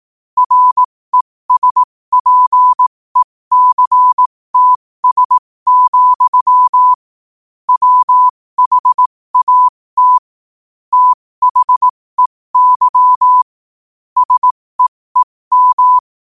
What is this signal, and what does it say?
Morse code, heard as a keyed tone, about 18 wpm, spaced as for 15 wpm, 1000 Hz. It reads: RESPECTS, WHAT THEY SEEM
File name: qotdfiles / 15wpm